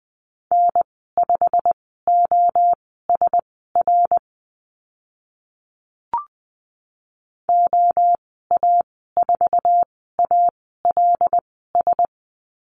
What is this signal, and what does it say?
Morse code, heard as a keyed tone, about 20 wpm, spaced as for 15 wpm, 700 Hz. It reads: N5OSR OA4ALS
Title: 088 Lesson 15g - Number 5 - Call Signs 20wpm